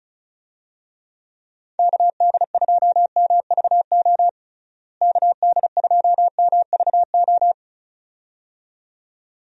Morse code, heard as a keyed tone, about 34 wpm, 700 Hz. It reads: KD2MVO KD2MVO
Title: USA Call Signs - 6 Characters - Repeated 2x - 35wpm